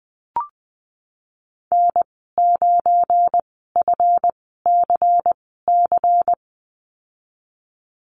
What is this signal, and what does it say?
Morse code, heard as a keyed tone, about 20 wpm, spaced as for 15 wpm, 700 Hz. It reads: N9FCC
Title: 207 Lesson 33g - Letter j - Call Signs 20wpm